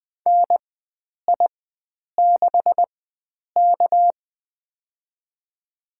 Morse code, heard as a keyed tone, about 20 wpm, spaced as for 9 wpm, 700 Hz. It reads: NI6K
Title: USA Call Signs - 4x Character Spacing - 20wpm